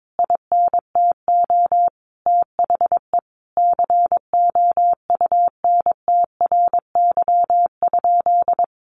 Morse code, heard as a keyed tone, about 22 wpm, 700 Hz. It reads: INTO THE COUNTRY?
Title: Sentences from Top 1400 Words 22wpm